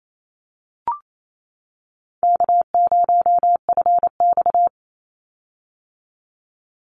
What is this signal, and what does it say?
Morse code, heard as a keyed tone, about 28 wpm, 700 Hz. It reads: K0FX
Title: Easy Call Signs - 28wpm